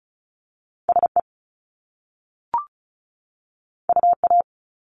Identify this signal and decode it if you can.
Morse code, heard as a keyed tone, about 36 wpm, 700 Hz. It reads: SE UA
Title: Top Two Letter Combinations in Words 35wpm